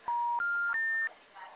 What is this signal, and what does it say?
Special information tone.
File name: out- -2010- - - .178